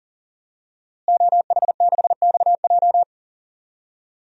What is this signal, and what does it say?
Morse code, heard as a keyed tone, about 38 wpm, 700 Hz. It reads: OH6XJ
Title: DXCC-100 Call Signs - 5 Characters - Repeated 2x - 40wpm